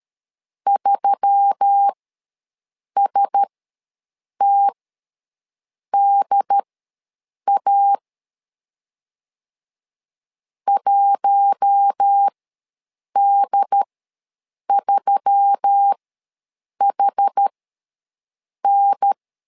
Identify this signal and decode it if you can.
Morse code, heard as a keyed tone, about 13 wpm, spaced as for 6 wpm, 800 Hz. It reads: 3STDA 1D3HN
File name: LEZIONI / 4 Q1S3 ws 12wpm 2038